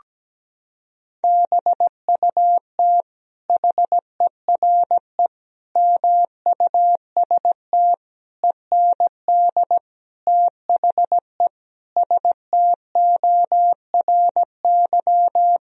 Morse code, heard as a keyed tone, about 17 wpm, 700 Hz. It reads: BUT HERE MUST END THE STORY
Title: Sentences from Top 1000 Words - Review Part 1 - 17wpm